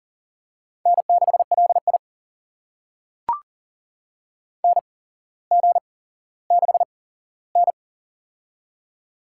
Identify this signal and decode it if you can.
Morse code, heard as a keyed tone, about 38 wpm, spaced as for 17 wpm, 700 Hz. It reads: N6LI NG6N
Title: USA Call Signs - 4 Characters - 8x Character Spacing - 40wpm